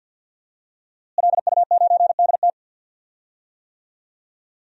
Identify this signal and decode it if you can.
Morse code, heard as a keyed tone, about 48 wpm, 700 Hz. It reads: LU9DT
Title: DXCC-100 Call Signs - 5 Characters - Repeated 2x - 50wpm